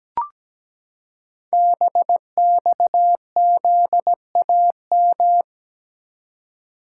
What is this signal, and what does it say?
Morse code, heard as a keyed tone, about 17 wpm, 700 Hz. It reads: BXZAM
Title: Code Groups - 5 Letters - 17wpm